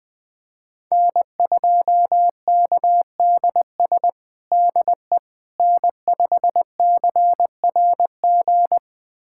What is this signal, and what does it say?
Morse code, heard as a keyed tone, about 19 wpm, 700 Hz. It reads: N2KDS DE N5CRG
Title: USA Callsign DE Callsign - 5 Characters - 20wpm